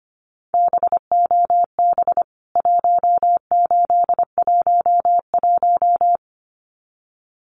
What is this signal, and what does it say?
Morse code, heard as a keyed tone, about 25 wpm, 700 Hz. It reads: BOB 1811
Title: CWT Exchange 25wpm